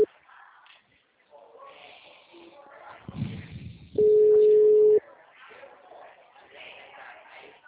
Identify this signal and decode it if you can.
Ringback tone (Europe, 425 Hz), with speech.